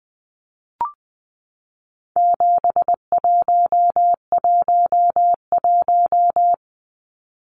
Morse code, heard as a keyed tone, about 20 wpm, 700 Hz. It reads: 7111
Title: Numbers - 4-Digit 20wpm